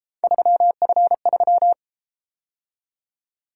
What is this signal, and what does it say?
Morse code, heard as a keyed tone, about 30 wpm, 700 Hz. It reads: 3F3